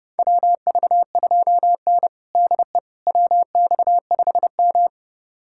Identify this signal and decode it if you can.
Morse code, heard as a keyed tone, about 30 wpm, 700 Hz. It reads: WV2N DE WX5M
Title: USA Callsign DE Callsign - 4 Characters - 30wpm